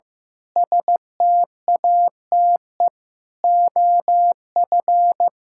Morse code, heard as a keyed tone, about 16 wpm, 700 Hz. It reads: STATE OF
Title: Pangram Challenge v2 - 15wpm